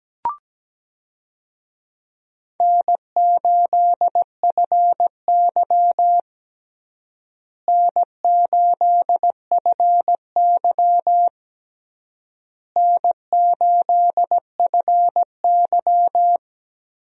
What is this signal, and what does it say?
Morse code, heard as a keyed tone, about 17 wpm, 700 Hz. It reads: N8FY N8FY N8FY
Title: USA Call Signs - 4 Characters - Repeated 3x - 17wpm